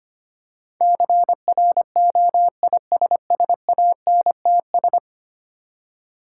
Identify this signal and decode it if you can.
Morse code, heard as a keyed tone, about 25 wpm, 700 Hz. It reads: CROISSANTS
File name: practice / 25wpm / Difficult Words with Hints - 25wpm